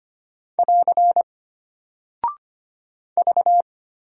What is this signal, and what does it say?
Morse code, heard as a keyed tone, about 25 wpm, 700 Hz. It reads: <AR> V
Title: Single Character (Letter:Number:Punctuation:Prosign) Alt 25wpm